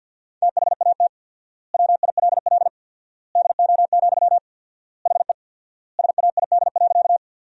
Morse code, heard as a keyed tone, about 52 wpm, 700 Hz. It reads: THAT WILL DO, HE SAID.
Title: 4x Word-Spacing - Sentences from Top 100 Words - 50wpm